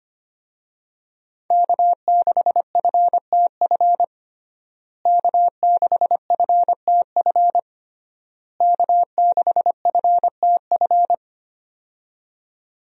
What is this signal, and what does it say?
Morse code, heard as a keyed tone, about 25 wpm, 700 Hz. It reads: K6FTF K6FTF K6FTF
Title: USA Call Signs - 5 Characters - Repeated 3x - 25wpm